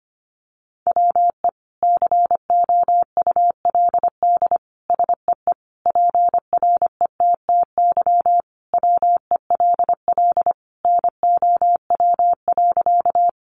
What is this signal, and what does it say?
Morse code, heard as a keyed tone, about 25 wpm, 700 Hz. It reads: WE COULD SEE PRETTY WELL NOW.
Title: Sentences from Top 1000 Words 25wpm